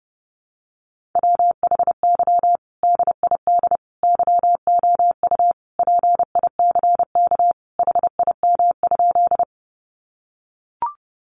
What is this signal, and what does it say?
Morse code, heard as a keyed tone, about 30 wpm, 700 Hz. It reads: WHY DID YOU PICK HIM?
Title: Sentences from Top 1000 Words 30wpm